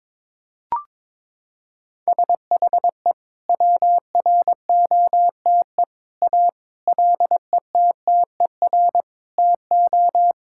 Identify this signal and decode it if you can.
Morse code, heard as a keyed tone, about 22 wpm, 700 Hz. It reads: SHE WROTE A LETTER TO
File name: practice / 22wpm / Sentences with 2-Word Phrases - 22wpm